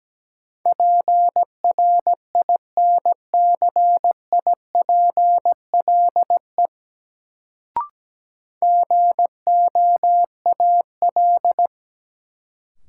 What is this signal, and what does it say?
Morse code, heard as a keyed tone, about 17 wpm, 700 Hz. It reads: PRINCIPLE GOAL
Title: Sentences from Top 1400 Words - New Word Drill Alt - 17wpm